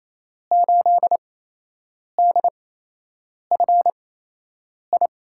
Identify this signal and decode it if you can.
Morse code, heard as a keyed tone, about 28 wpm, spaced as for 8 wpm, 700 Hz. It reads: 8DFI
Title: USA Call Signs - 8x Character Spacing - 28wpm